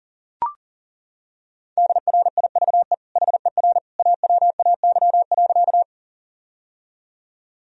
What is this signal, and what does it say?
Morse code, heard as a keyed tone, about 40 wpm, 700 Hz. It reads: DRIVE HER AWAY.
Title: Sentences from Top 1000 Words - Review Part 2 - 40wpm